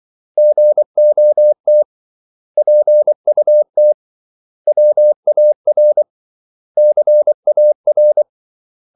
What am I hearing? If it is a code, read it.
Morse code, heard as a keyed tone, about 24 wpm, 600 Hz. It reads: GOT PUT WAR CAR
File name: CW / Practice / 500MostCommonEnglishWords24